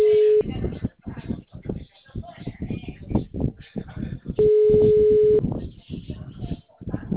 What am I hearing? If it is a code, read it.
Ringback tone (Europe, 425 Hz), with speech.
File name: out- -2014- - - .49617